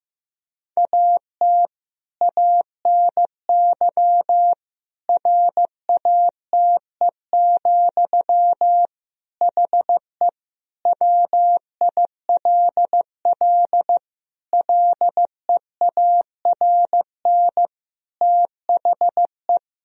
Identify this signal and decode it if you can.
Morse code, heard as a keyed tone, about 15 wpm, 700 Hz. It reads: AT ANY RATE, HE WILL LEARN THE
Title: Sentences from Top 1000 Words - Review Part 4 - 15wpm